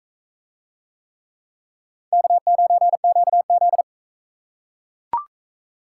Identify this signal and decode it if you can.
Morse code, heard as a keyed tone, about 38 wpm, 700 Hz. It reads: K9QZ